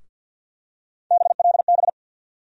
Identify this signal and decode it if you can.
Morse code, heard as a keyed tone, about 50 wpm, 700 Hz. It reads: BLB
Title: USA Call Sign Postfixes - 3 Letters 50wpm